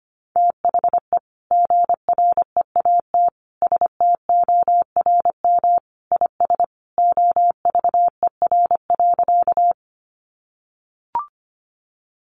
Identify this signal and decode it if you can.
Morse code, heard as a keyed tone, about 25 wpm, 700 Hz. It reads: THE GREAT STORM IS OVER.
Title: Sentences from Top 1500 Words 25wpm